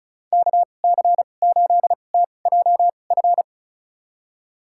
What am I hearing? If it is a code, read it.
Morse code, heard as a keyed tone, about 34 wpm, spaced as for 25 wpm, 700 Hz. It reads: KC8TJF
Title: USA Call Signs - 2x Character Spacing - 35wpm